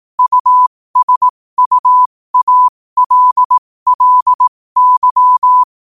Morse code, heard as a keyed tone, about 18 wpm, spaced as for 15 wpm, 1000 Hz. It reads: USUALLY
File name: qotdfiles / 15wpm